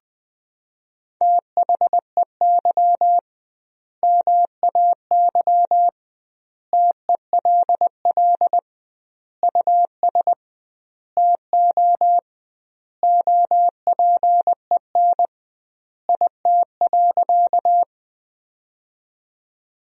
Morse code, heard as a keyed tone, about 21 wpm, 700 Hz. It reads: THEY MAY TELL US TO OPEN IT.
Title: 2x Word-Spacing - Sentences from Top 300 Words - 20wpm